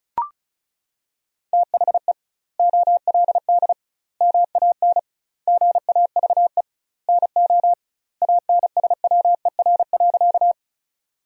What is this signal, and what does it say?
Morse code, heard as a keyed tone, about 36 wpm, 700 Hz. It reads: THE OLD MAN GAVE NO ANSWER.
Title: 2x Word-Spacing - Sentences from Top 500 Words - 35wpm